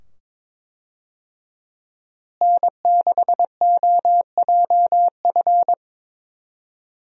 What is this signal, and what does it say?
Morse code, heard as a keyed tone, about 22 wpm, 700 Hz. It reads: N6OJF
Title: USA Call Signs - 5 Characters - Repeated 2x - 22wpm